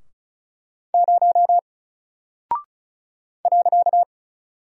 Morse code, heard as a keyed tone, about 36 wpm, 700 Hz. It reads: 0 .
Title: Single Character (Letter:Number:Punctuation:Prosign) Alt 35wpm